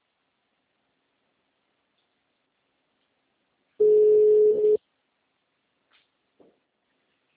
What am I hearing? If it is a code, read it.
Ringback tone (Europe, 425 Hz).
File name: out- -2006- - - .69